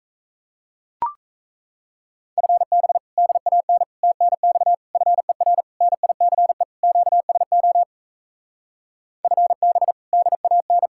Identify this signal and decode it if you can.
Morse code, heard as a keyed tone, about 38 wpm, 700 Hz. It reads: FB DAN TNX FER NICE QSO FB DAN